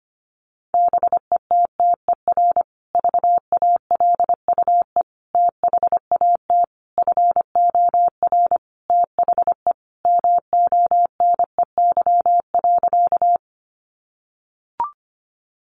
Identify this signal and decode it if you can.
Morse code, heard as a keyed tone, about 25 wpm, 700 Hz. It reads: BETTER VALUE THAT FOR THE MONEY.
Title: Sentences from Top 600 Words 25wpm